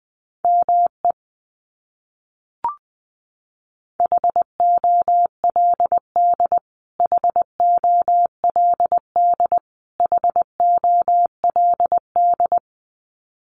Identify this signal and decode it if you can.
Morse code, heard as a keyed tone, about 20 wpm, 700 Hz. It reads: ME HOLD HOLD HOLD
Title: Top 500 Words Repeated 20wpm